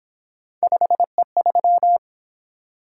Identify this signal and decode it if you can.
Morse code, heard as a keyed tone, about 27 wpm, 700 Hz. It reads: 5E3
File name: Speed-Racing - Three Letter-Number Combinations - 22wpm